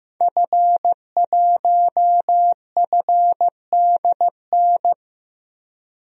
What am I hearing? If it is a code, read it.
Morse code, heard as a keyed tone, about 15 wpm, 700 Hz. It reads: F1FDN
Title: DXCC-100 Call Signs - 5 Characters - 15wpm